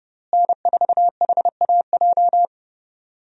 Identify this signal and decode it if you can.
Morse code, heard as a keyed tone, about 30 wpm, 700 Hz. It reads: N4HAJ
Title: USA Call Signs - 5 Characters - Repeated 3x - 30wpm